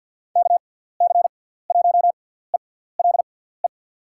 Morse code, heard as a keyed tone, about 47 wpm, spaced as for 17 wpm, 700 Hz. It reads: KC1ELE
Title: USA Call Signs - 6 Characters - 6x Character Spacing - 50wpm